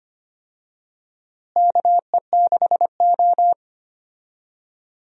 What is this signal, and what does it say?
Morse code, heard as a keyed tone, about 25 wpm, 700 Hz. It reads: KE6O
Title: USA Call Signs - 4 Characters - Repeated 3x - 25wpm